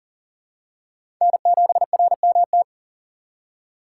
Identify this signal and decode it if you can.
Morse code, heard as a keyed tone, about 37 wpm, 700 Hz. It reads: N7RMT
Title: USA Call Signs - 2x Character Spacing - 40wpm